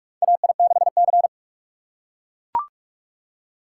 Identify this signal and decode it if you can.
Morse code, heard as a keyed tone, about 44 wpm, 690 Hz. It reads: AI6C
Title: USA Call Signs - 4 Characters - 3x Character Spacing - 45wpm